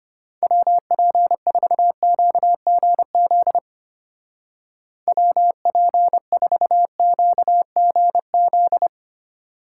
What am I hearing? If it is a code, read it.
Morse code, heard as a keyed tone, about 31 wpm, 700 Hz. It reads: WP4QGZ WP4QGZ
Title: Speed Racing - USA Call Signs 25wpm